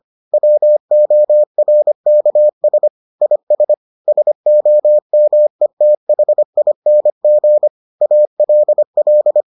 Morse code, heard as a keyed tone, about 25 wpm, 600 Hz. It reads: WORKS IS SOMETHING ALL